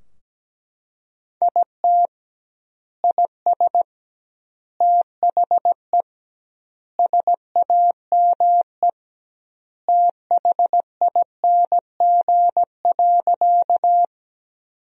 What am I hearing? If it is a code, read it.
Morse code, heard as a keyed tone, about 17 wpm, 700 Hz. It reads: IT IS THE SAME THING.
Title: 2x Word-Spacing - Sentences from Top 200 Words - 17wpm